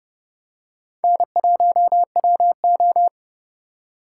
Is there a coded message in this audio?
Yes: Morse code from 1.0 to 3.1 s, heard as a keyed tone, about 31 wpm, 700 Hz. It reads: N1WO